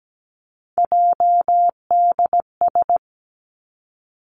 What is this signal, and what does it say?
Morse code, heard as a keyed tone, about 17 wpm, 700 Hz. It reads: JDS
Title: Three Challenging Letter-Number Combinations 17wpm